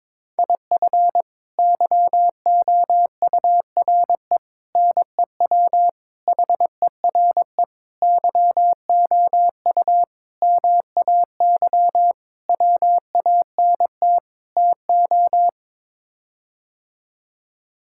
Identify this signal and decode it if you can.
Morse code, heard as a keyed tone, about 21 wpm, 700 Hz. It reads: IF YOURE NEW HERE YOU MAY WANT TO
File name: practice / 22wpm / Sets of 8 Words - 22wpm